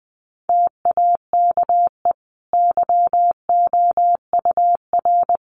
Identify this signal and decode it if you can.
Morse code, heard as a keyed tone, about 20 wpm, 700 Hz. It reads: TAKE YOUR
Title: Sentences from Top 1500 Words 20wpm